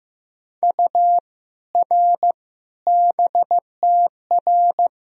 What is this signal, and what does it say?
Morse code, heard as a keyed tone, about 15 wpm, 700 Hz. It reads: U R BTR
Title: Sentences from Top 400 Words and Abbreviations 15wpm